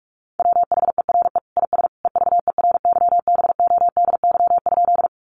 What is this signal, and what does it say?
Morse code, heard as a keyed tone, about 44 wpm, 700 Hz. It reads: WHERE IS EVERYBODY?
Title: Sentences from Top 1400 Words 45wpm